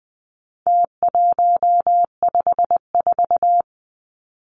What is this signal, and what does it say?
Morse code, heard as a keyed tone, about 20 wpm, 700 Hz. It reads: T154
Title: Numbers - Contest Style - 4-Digit 20wpm